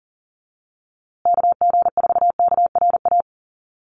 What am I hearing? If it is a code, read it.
Morse code, heard as a keyed tone, about 40 wpm, 700 Hz. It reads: KG4KRA
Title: USA Call Signs Repeated 2x 40wpm